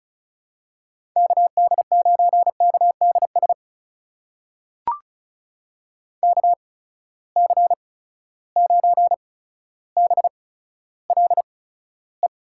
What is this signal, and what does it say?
Morse code, heard as a keyed tone, about 36 wpm, spaced as for 15 wpm, 700 Hz. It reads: KD9KDS KC9BLE
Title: USA Call Signs - 6 Characters - 8x Character Spacing - 35wpm